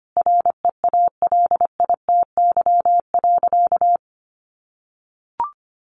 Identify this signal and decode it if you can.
Morse code, heard as a keyed tone, about 25 wpm, 700 Hz. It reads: REALITY.
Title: Sentences from Top 1200 Words 25wpm